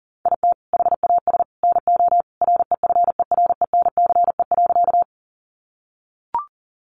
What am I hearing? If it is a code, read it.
Morse code, heard as a keyed tone, about 40 wpm, 700 Hz. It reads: IT HAS NO REFERENCE.
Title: Sentences from Top 1200 Words 40wpm